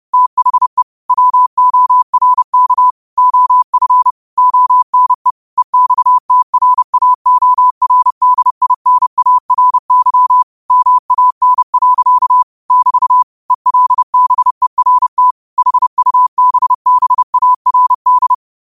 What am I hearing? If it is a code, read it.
Morse code, heard as a keyed tone, about 30 wpm, 1000 Hz. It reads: THE WORK OF ONE EXTRAORDINARY MAN. <BT> ELBERT HUBBARD
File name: qotdfiles / 30wpm